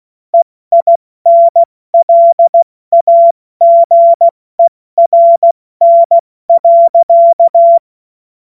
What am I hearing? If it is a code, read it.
Morse code, heard as a keyed tone, about 16 wpm, 680 Hz. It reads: EINLAGERN.
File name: Morsekurs / Witze / Witz 18. Mai 2021, 70 F80